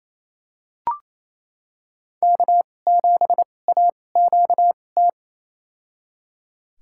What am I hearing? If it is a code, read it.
Morse code, heard as a keyed tone, about 28 wpm, spaced as for 20 wpm, 700 Hz. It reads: K7AQT
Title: USA Call Signs - 5 Characters - 2x Character Spacing - 28wpm